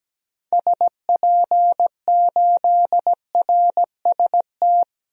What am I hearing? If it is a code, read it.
Morse code, heard as a keyed tone, about 17 wpm, 700 Hz. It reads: SP8RST
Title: DXCC-100 Call Signs - 6 Characters - 17wpm